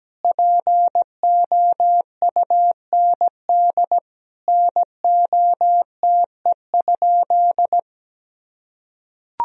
Morse code, heard as a keyed tone, about 17 wpm, 700 Hz. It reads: POUND NOTE?
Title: Sentences from Top 1000 Words - Review Part 2 - 17wpm